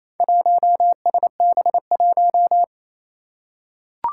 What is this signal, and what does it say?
Morse code, heard as a keyed tone, about 28 wpm, 700 Hz. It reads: 1SB1
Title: Four Challenging Letter-Number Combinations 28wpm